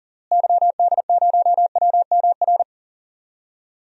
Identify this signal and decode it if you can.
Morse code, heard as a keyed tone, about 40 wpm, 700 Hz. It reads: YD0WMR
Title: DXCC-100 Call Signs - 6 Characters - Repeated 3x - 40wpm